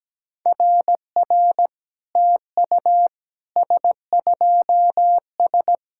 Morse code, heard as a keyed tone, about 17 wpm, 700 Hz. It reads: RR TU S2S
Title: SOTA QSO Practice 17wpm